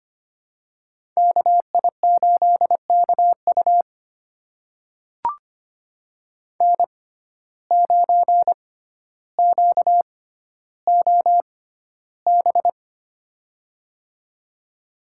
Morse code, heard as a keyed tone, about 25 wpm, spaced as for 13 wpm, 700 Hz. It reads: KI8KU N9QOB
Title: USA Call Signs - 5 Characters - 6x Character Spacing - 25wpm